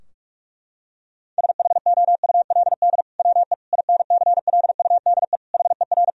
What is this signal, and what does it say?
Morse code, heard as a keyed tone, about 46 wpm, 700 Hz. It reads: SHOULD WE INCLUDE HER
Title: Sentences from Top 1000 Words - Review Part 3 - 45wpm